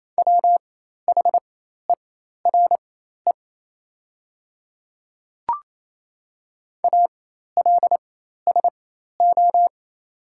Morse code, heard as a keyed tone, about 28 wpm, spaced as for 13 wpm, 700 Hz. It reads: WHERE ALSO
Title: 4x Character Spacing - Top 200 Words - 28wpm